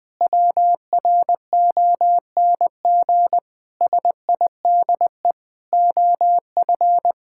Morse code, heard as a keyed tone, about 20 wpm, 700 Hz. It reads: WRONG SIDE OF
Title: Sentences with 6-Word Phrases - 20wpm